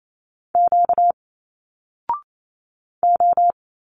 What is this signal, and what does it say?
Morse code, heard as a keyed tone, about 28 wpm, 700 Hz. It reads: Q O
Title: Single Letters 28wpm